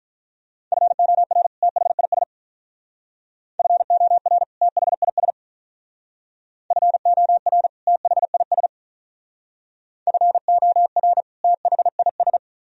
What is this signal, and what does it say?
Morse code, heard as a keyed tone, about 43 wpm, 700 Hz. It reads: FOR THIS FOR THIS FOR THIS FOR THIS
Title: Speed-Racing - Sets of 2 Words 35wpm